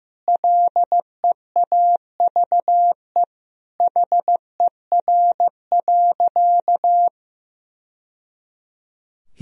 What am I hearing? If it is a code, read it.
Morse code, heard as a keyed tone, about 15 wpm, 700 Hz. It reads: LEAVE HER.
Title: Sentences from Top 1000 Words - Review Part 1 - 15wpm